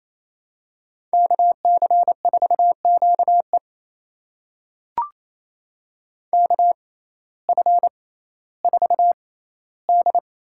Morse code, heard as a keyed tone, about 28 wpm, spaced as for 15 wpm, 700 Hz. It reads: KC4QE KF4D
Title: USA Call Signs - 6x Character Spacing - 28wpm